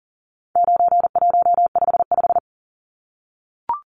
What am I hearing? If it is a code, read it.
Morse code, heard as a keyed tone, about 40 wpm, 700 Hz. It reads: 9155
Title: Numbers - 4-Digit 40wpm